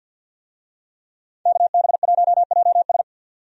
Morse code, heard as a keyed tone, about 50 wpm, 700 Hz. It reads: KB1JS
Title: USA Call Signs - 5 Characters - 50wpm